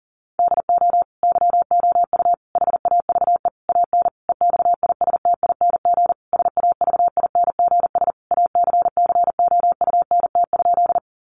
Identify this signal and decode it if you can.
Morse code, heard as a keyed tone, about 37 wpm, 700 Hz. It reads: DO YOU HAVE AN EXISTING SAVINGS ACCOUNT?
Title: Sentences from Top 1500 Words 40wpm